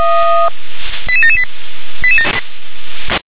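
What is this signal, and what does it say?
Five-tone selective call (ZVEI), heard as quick runs of five tones, tones of about 70 ms. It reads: 88778
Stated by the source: ZVEI five-tone selective call 88953 received on SDR.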